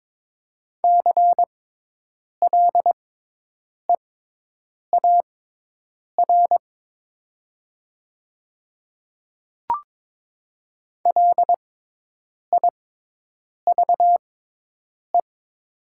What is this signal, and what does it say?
Morse code, heard as a keyed tone, about 22 wpm, spaced as for 8 wpm, 700 Hz. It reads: CLEAR LIVE
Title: 6x Character Spacing - Top 500 Words - 22wpm